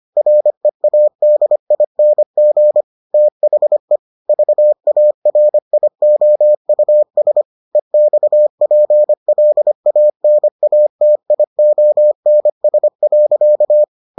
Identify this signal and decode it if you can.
Morse code, heard as a keyed tone, about 25 wpm, 600 Hz. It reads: READING THE VARIOUS EXPLANATIONS.